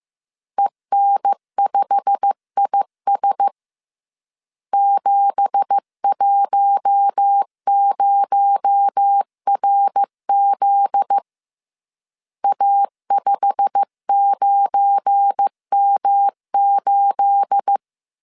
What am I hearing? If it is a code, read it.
Morse code, heard as a keyed tone, about 15 wpm, 800 Hz. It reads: EN5IS 710RZ A59M8
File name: CW / DET / 1 9 1500C S C 14wpm 3245